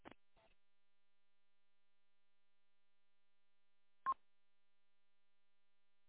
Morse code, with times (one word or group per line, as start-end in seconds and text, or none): none